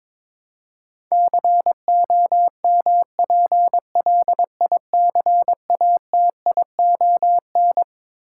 Morse code, heard as a keyed tone, about 22 wpm, 700 Hz. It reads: COMPLICATION
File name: Twelve Letter Words - 22wpm